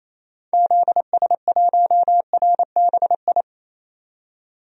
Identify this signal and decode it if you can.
Morse code, heard as a keyed tone, about 28 wpm, 700 Hz. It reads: ZS1RBI
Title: DXCC-100 Call Signs - 6 Characters - 28wpm